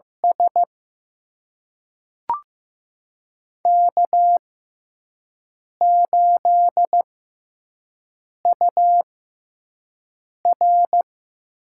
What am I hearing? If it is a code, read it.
Morse code, heard as a keyed tone, about 15 wpm, spaced as for 5 wpm, 700 Hz. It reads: S K8UR
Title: USA Call Signs - 4 Characters - 6x Character Spacing - 15wpm